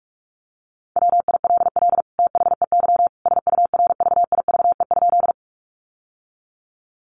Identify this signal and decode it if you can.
Morse code, heard as a keyed tone, about 44 wpm, 700 Hz. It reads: WILL THEY SURVIVE?
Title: Sentences from Top 1300 Words 45wpm